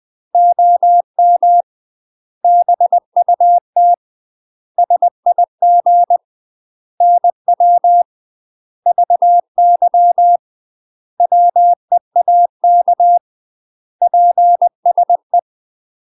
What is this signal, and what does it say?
Morse code, heard as a keyed tone, about 20 wpm, 700 Hz. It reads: OM BUT SIG NW VY WEAK PSE
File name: UK MASTER QSOs TRAINING / 20WPM / UK MASTER QSOs 0063